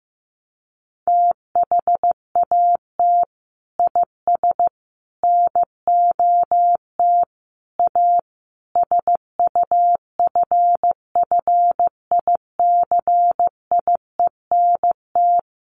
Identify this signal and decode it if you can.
Morse code, heard as a keyed tone, about 14 wpm, 700 Hz. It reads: THAT IS NOT A SUFFICIENT
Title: Sentences from Top 1400 Words 15wpm